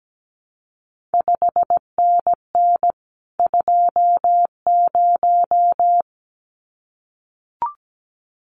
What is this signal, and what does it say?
Morse code, heard as a keyed tone, about 17 wpm, 700 Hz. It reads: 5NN 20
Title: CQ WW DX Contest Exchange - 17wpm